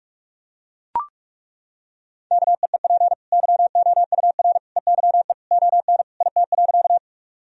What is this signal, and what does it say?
Morse code, heard as a keyed tone, about 41 wpm, 700 Hz. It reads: KEEP YOUR EYE ON IT.
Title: Sentences from Top 1000 Words - Review Part 1 - 45wpm